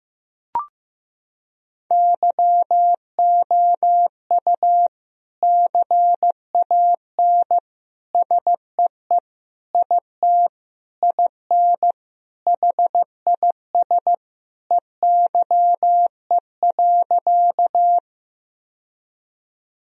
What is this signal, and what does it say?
Morse code, heard as a keyed tone, about 15 wpm, 700 Hz. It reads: YOU CAN SEE IT IN HIS EYE.
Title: Sentences from Top 1000 Words - Review Part 1 - 15wpm